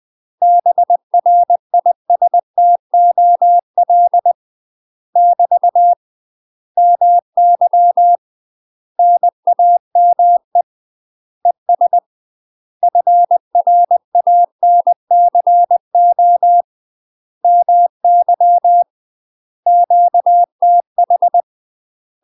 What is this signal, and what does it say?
Morse code, heard as a keyed tone, about 20 wpm, 700 Hz. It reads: BRISTOL <BT> MY NAME ES FRANCO MY QTH
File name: UK MASTER QSOs TRAINING / 20WPM / UK MASTER QSOs 0053